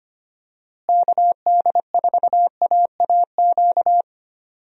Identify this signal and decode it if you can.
Morse code, heard as a keyed tone, about 25 wpm, 700 Hz. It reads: KD4AAQ
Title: USA Call Signs - 6 Characters - Repeated 3x - 25wpm